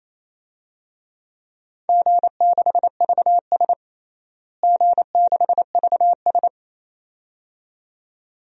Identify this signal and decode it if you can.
Morse code, heard as a keyed tone, about 28 wpm, 700 Hz. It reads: G6VS G6VS
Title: DXCC-100 Call Signs - 4 Characters - Repeated 2x - 28wpm